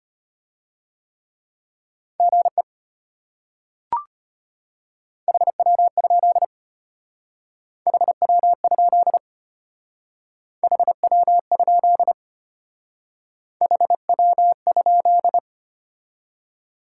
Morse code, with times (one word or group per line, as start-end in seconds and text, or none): 2.2-2.6 GE
5.3-6.5 HW?
7.9-9.2 HW?
10.6-12.1 HW?
13.6-15.4 HW?